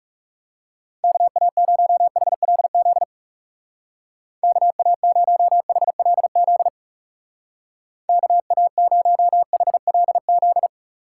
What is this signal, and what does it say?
Morse code, heard as a keyed tone, about 37 wpm, 700 Hz. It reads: KA0HLZ KA0HLZ KA0HLZ
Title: Speed Racing - USA 2x3 Call Signs 30wpm